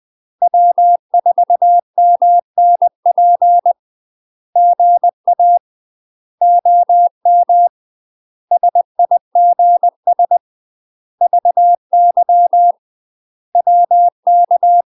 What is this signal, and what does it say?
Morse code, heard as a keyed tone, about 20 wpm, 700 Hz. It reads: W4MNP GA OM SIGS VY WK
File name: UK MASTER QSOs TRAINING / 20WPM / UK MASTER QSOs 0021